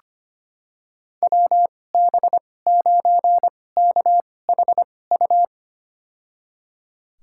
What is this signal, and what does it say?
Morse code, heard as a keyed tone, about 25 wpm, spaced as for 18 wpm, 700 Hz. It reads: WB9KHU
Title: USA Call Signs - 6 Characters - 2x Character Spacing - 25wpm